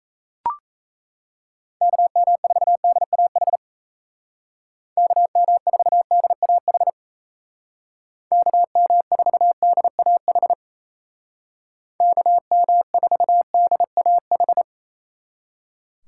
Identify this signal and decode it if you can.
Morse code, heard as a keyed tone, about 37 wpm, 700 Hz. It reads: KM4DAH KM4DAH KM4DAH KM4DAH